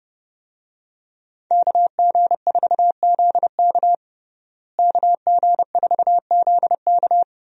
Morse code, heard as a keyed tone, about 30 wpm, 700 Hz. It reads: KG4ZK KG4ZK
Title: USA Call Signs - 5 Characters - Repeated 3x - 30wpm